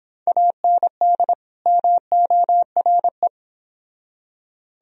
Morse code, heard as a keyed tone, about 25 wpm, 700 Hz. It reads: AND MORE